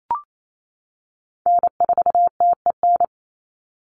Morse code, heard as a keyed tone, about 28 wpm, 700 Hz. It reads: N4TEN
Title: USA Call Signs 28wpm